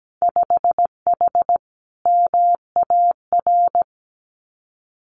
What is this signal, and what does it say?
Morse code, heard as a keyed tone, about 17 wpm, 700 Hz. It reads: 5H MAR
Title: Winter Field Day Exchanges Encore 17wpm